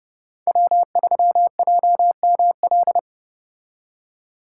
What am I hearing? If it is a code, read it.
Morse code, heard as a keyed tone, about 34 wpm, 700 Hz. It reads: W3JML